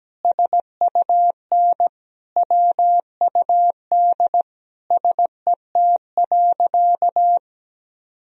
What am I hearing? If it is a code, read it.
Morse code, heard as a keyed tone, about 17 wpm, 700 Hz. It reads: SUN WUD SET.
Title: Sentences from Top 400 Words and Abbreviations 17wpm